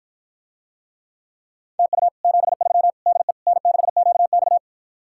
Morse code, heard as a keyed tone, about 54 wpm, 700 Hz. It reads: TU 73 DE N6QX